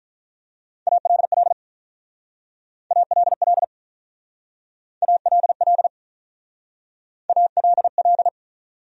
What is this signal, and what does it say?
Morse code, heard as a keyed tone, about 45 wpm, 700 Hz. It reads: ALL ALL ALL ALL